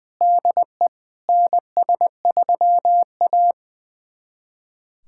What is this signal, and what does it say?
Morse code, heard as a keyed tone, about 20 wpm, 700 Hz. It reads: DE NS3A
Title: USA Callsign DE Callsign - 4 Characters - 20wpm